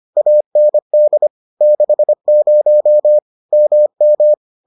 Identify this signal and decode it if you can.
Morse code, heard as a keyed tone, about 25 wpm, 600 Hz. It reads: AND 60 MM